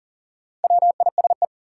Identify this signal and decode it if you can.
Morse code, heard as a keyed tone, about 40 wpm, 700 Hz. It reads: WISE
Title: Sentences from Top 1600 Words - New Word Drill - 40wpm